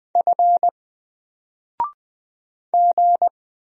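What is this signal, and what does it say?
Morse code, heard as a keyed tone, about 20 wpm, 700 Hz. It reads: F G
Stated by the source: Single letter and number Morse drill at 20 wpm, from N4LCD.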